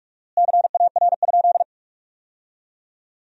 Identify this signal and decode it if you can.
Morse code, heard as a keyed tone, about 42 wpm, 700 Hz. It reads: CAR?
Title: 4x Word-Spacing - Sentences from Top 300 Words - 45wpm